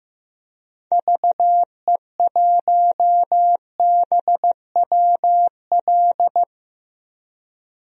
Morse code, heard as a keyed tone, about 15 wpm, 700 Hz. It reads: VE1BWL
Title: DXCC-100 Call Signs - 6 Characters - 15wpm